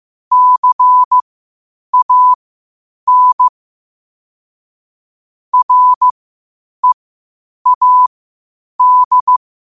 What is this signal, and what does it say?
Morse code, heard as a keyed tone, about 15 wpm, spaced as for 8 wpm, 1000 Hz. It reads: CAN READ